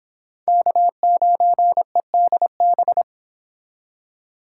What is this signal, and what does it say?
Morse code, heard as a keyed tone, about 30 wpm, 700 Hz. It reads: K9EDB